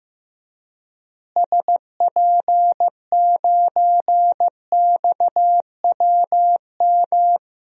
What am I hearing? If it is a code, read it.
Morse code, heard as a keyed tone, about 15 wpm, 700 Hz. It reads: SP9XWM
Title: DXCC-100 Call Signs - 6 Characters - Repeated 3x - 15wpm